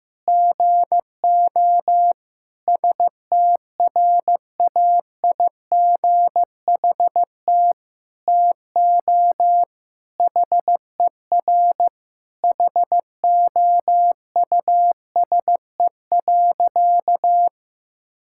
Morse code, heard as a keyed tone, about 15 wpm, 700 Hz. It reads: GO STRAIGHT TO HER HOUSE.
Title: Sentences from Top 1000 Words - Review Part 6 - 15wpm